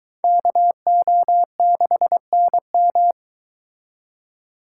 Morse code, heard as a keyed tone, about 22 wpm, 700 Hz. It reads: KO6NM